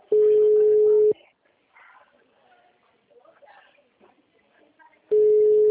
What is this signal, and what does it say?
Ringback tone (Europe, 425 Hz), with speech.